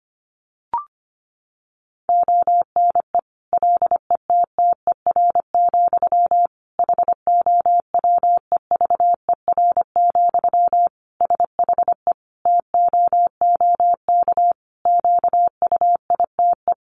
Morse code, heard as a keyed tone, about 25 wpm, 700 Hz. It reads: ONE LETTER, HOWEVER, SHE TOOK QUITE
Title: Sentences from Top 1200 Words 25wpm